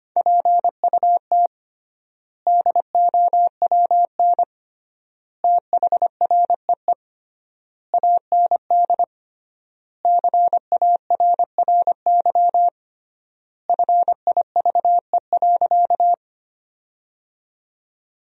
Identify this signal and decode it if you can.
Morse code, heard as a keyed tone, about 25 wpm, 700 Hz. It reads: PUT DOWN THREE AND CARRY FIVE.
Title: 3x Word-Spacing - Sentences from Top 400 Words - 25wpm